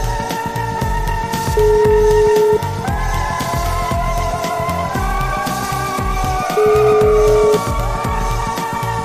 Ringback tone (Europe, 425 Hz), with noise.